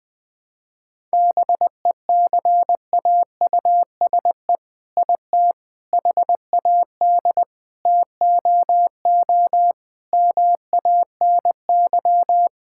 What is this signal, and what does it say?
Morse code, heard as a keyed tone, about 20 wpm, 700 Hz. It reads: BECAUSE IT HAD TOO MANY
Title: All-Age Appropriate Jokes - 20wpm